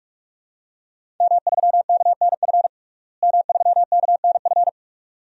Morse code, heard as a keyed tone, about 48 wpm, 700 Hz. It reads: M3KNF M3KNF
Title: DXCC-100 Call Signs - 5 Characters - Repeated 2x - 45wpm